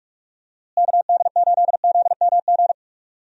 Morse code, heard as a keyed tone, about 46 wpm, 700 Hz. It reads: KD8ZMG
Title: USA Call Signs - 6 Characters - Repeated 2x - 45wpm